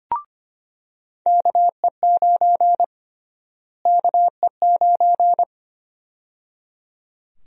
Morse code, heard as a keyed tone, about 25 wpm, 700 Hz. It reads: KE9 KE9
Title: USA Call Sign Prefixes - Repeated 2x - 25wpm